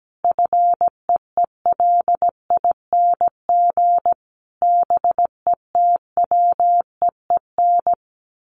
Morse code, heard as a keyed tone, about 17 wpm, 700 Hz. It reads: FEELING BETWEEN
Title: Sentences from Top 1400 Words 17wpm